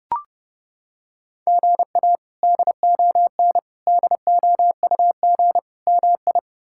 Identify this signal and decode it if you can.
Morse code, heard as a keyed tone, about 30 wpm, 700 Hz. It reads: GA DON DOUG MI
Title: K1USN SST QSO Practice - 30wpm